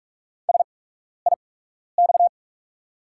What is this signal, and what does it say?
Morse code, heard as a keyed tone, about 44 wpm, spaced as for 12 wpm, 690 Hz. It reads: SIX
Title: 8x Character Spacing - Top 500 Words - 45wpm